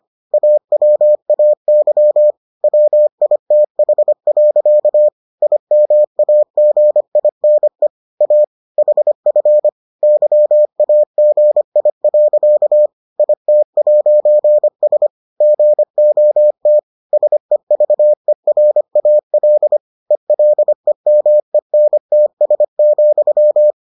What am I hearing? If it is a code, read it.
Morse code, heard as a keyed tone, about 25 wpm, 600 Hz. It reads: AWAY WITH. IMAGINE A HF YAGI. IT'S GOT SEVERAL ELEMENTS,